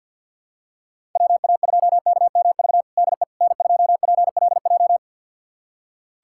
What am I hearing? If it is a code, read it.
Morse code, heard as a keyed tone, about 47 wpm, 700 Hz. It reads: WA2KMV DE N2PLJ
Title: USA Callsign DE Callsign 50wpm